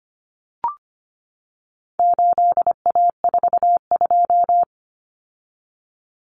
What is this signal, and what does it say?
Morse code, heard as a keyed tone, about 25 wpm, 700 Hz. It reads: 8A42
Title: Common Cut Numbers - 4-Digit 25wpm